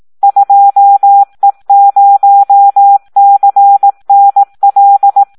Morse code, heard as a keyed tone, about 18 wpm, 800 Hz. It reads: 2E0CNL
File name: Audio practica CW / SD 1 18